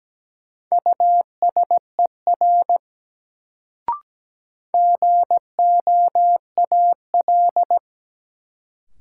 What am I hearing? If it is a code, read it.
Morse code, heard as a keyed tone, about 17 wpm, 700 Hz. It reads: USER GOAL
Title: Sentences from Top 1000 Words - Review Part 5 - New Word Drill - 17wpm